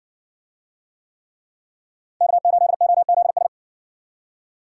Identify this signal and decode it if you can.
Morse code, heard as a keyed tone, about 58 wpm, 700 Hz. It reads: K8OZS